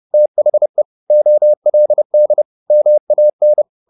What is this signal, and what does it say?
Morse code, heard as a keyed tone, about 30 wpm, 600 Hz. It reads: THE OLD MAN
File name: the old man and the sea / the old man and the sea 1 30wpm